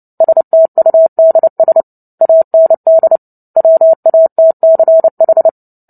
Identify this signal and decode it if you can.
Morse code, heard as a keyed tone, about 30 wpm, 650 Hz. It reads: STUDS AND WATCH
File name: Book Two / 03-Chapter 3 - 29 WPM